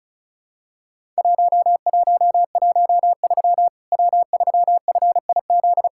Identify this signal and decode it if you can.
Morse code, heard as a keyed tone, about 34 wpm, 700 Hz. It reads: 1113 W3FIZ
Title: Worked All Europe DX Contest - QTC Exchange - 35wpm